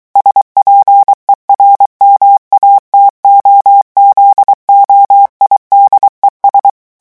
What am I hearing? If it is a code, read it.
Morse code, heard as a keyed tone, about 23 wpm, 800 Hz. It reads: SPERMATOZOIDES
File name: morse mp3 / revision9 citations humoristiques vitesse 22 mots minute